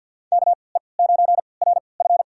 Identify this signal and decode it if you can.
Morse code, heard as a keyed tone, about 52 wpm, spaced as for 28 wpm, 700 Hz. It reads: KE9RF
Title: USA Call Signs - 5 Characters - 3x Character Spacing - 50wpm